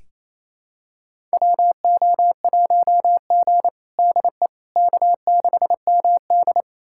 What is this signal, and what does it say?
Morse code, heard as a keyed tone, about 28 wpm, 700 Hz. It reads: WO1G DE K6MD
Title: USA Callsign DE Callsign - 4 Characters - 28wpm